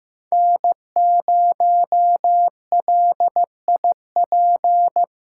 Morse code, heard as a keyed tone, about 15 wpm, 700 Hz. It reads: N0LIP
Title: USA Callsign DE Callsign - 5 Characters - 15wpm